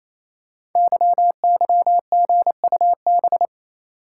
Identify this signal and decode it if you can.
Morse code, heard as a keyed tone, about 29 wpm, 700 Hz. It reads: YYGUB
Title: Code Groups - 5 Letters - 28wpm